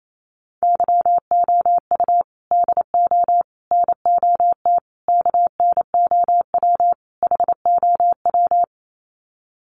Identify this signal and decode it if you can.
Morse code, heard as a keyed tone, about 28 wpm, 700 Hz. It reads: YOU DO NOT KNOW HOW
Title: Sets of 5 Words Encore 28wpm